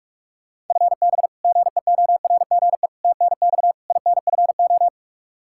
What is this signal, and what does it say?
Morse code, heard as a keyed tone, about 47 wpm, 700 Hz. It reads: FB GEORGE TNX INFO
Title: K1USN SST QSO Practice 45wpm